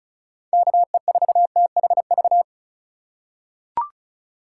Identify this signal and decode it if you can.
Morse code, heard as a keyed tone, about 34 wpm, 700 Hz. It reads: KE4THV
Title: USA Call Signs - 6 Characters - 4x Character Spacing - 35wpm